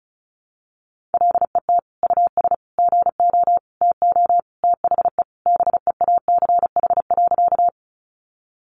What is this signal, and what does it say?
Morse code, heard as a keyed tone, about 34 wpm, 700 Hz. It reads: LET US GO TO THE BEACH.
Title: Sentences from Top 1500 Words 35wpm